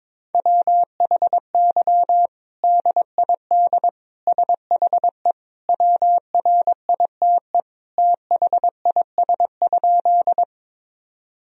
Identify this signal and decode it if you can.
Morse code, heard as a keyed tone, about 21 wpm, 700 Hz. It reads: WHY DID SHE WRITE THIS?
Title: Sentences from Top 1000 Words - Review Part 1 - 22wpm